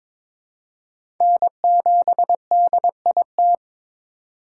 Morse code, heard as a keyed tone, about 21 wpm, 700 Hz. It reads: N7DIT
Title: USA Call Signs - 5 Characters - 4x Character Spacing - 22wpm